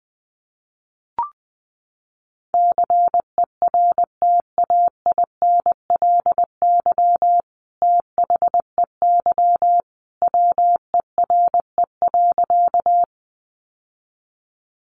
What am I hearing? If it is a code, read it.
Morse code, heard as a keyed tone, about 20 wpm, 700 Hz. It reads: CERTAINLY THEY WERE.
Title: Sentences from Top 900 Words 20wpm